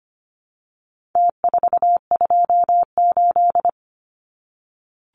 Morse code, heard as a keyed tone, about 25 wpm, 700 Hz. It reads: T428
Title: Numbers - Contest Style - 4-Digit 25wpm